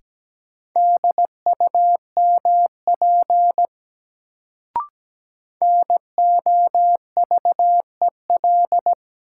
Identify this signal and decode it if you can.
Morse code, heard as a keyed tone, about 17 wpm, 700 Hz. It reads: DUMP NOVEL
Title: Sentences from Top 1300 Words - New Word Drill Alt - 17wpm